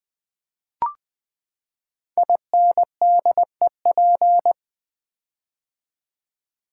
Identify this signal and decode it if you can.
Morse code, heard as a keyed tone, about 20 wpm, 700 Hz. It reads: INDEP
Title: Build A Word - 5 Letters At A Time - 12-Letter Words - 20wpm